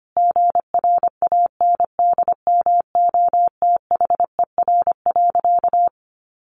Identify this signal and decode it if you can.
Morse code, heard as a keyed tone, about 25 wpm, 700 Hz. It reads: GRANDMOTHER.
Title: Sentences from Top 1200 Words 25wpm